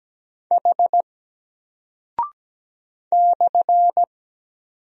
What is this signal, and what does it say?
Morse code, heard as a keyed tone, about 17 wpm, 700 Hz. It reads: H /
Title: Single Character (Letter:Number:Punctuation:Prosign) Alt 17wpm